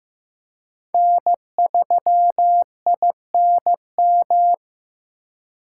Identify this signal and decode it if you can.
Morse code, heard as a keyed tone, about 18 wpm, 700 Hz. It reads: N3INM
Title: Speed Racing - USA 1x3 Call Signs 15wpm